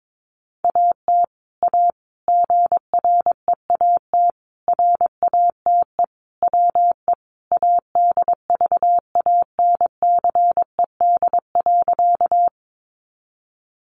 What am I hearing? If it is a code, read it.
Morse code, heard as a keyed tone, about 22 wpm, 700 Hz. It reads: AT A GREAT RATE WE ADVANCED.
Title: Sentences from Top 1200 Words 22wpm